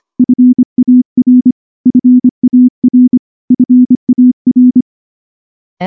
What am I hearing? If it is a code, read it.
Morse code, heard as a keyed tone, about 25 wpm, 250 Hz. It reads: FAR FAR FAR
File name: morse / morse-training-25WPM